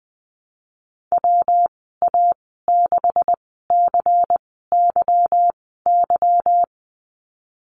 Morse code, heard as a keyed tone, about 20 wpm, spaced as for 15 wpm, 700 Hz. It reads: WA6CYY
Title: 193 Lesson 31g - Letter v - Call Signs 20wpm